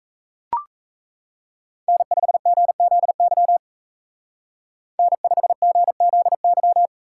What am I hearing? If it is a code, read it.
Morse code, heard as a keyed tone, about 33 wpm, 700 Hz. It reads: N5GZY N5GZY
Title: Speed Racing - USA 1x3 Call Signs 28wpm